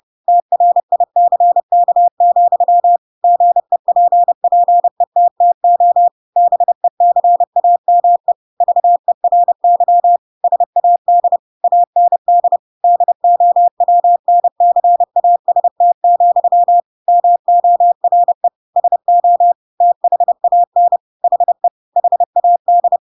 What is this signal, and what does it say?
Morse code, heard as a keyed tone, about 30 wpm, 700 Hz. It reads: TRICK, GEPPETTO BECAME VERY SAD AND DOWNCAST, MORE SO THAN HE HAD
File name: Pinocchio / Pinocchio ch3 30 30 ik1hge